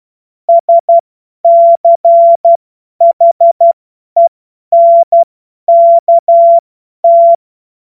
Morse code, heard as a keyed tone, about 12 wpm, spaced as for 10 wpm, 680 Hz. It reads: SCHENKT
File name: Morsekurs / Witze / Witz 07. Juni 2022 50 F70